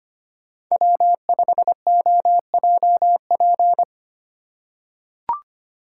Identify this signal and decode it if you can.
Morse code, heard as a keyed tone, about 25 wpm, 700 Hz. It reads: W5OJP
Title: USA Call Signs - 8x Character Spacing - 25wpm